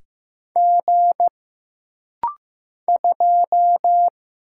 Morse code, heard as a keyed tone, about 15 wpm, 700 Hz. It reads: G 2
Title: Single Letter-Number Alt - Rapid-Fire 15wpm